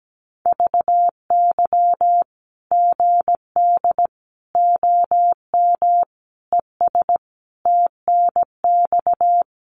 Morse code, heard as a keyed tone, about 17 wpm, 700 Hz. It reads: VY GD OM ES TNX
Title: Common QSO Phrases 17wpm